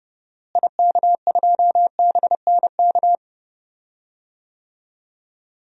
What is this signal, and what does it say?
Morse code, heard as a keyed tone, about 30 wpm, 700 Hz. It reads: IK2BNK
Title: DXCC-100 Call Signs - 6 Characters - Repeated 2x - 30wpm